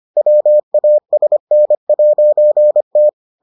Morse code, heard as a keyed tone, about 25 wpm, 600 Hz. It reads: WASN'T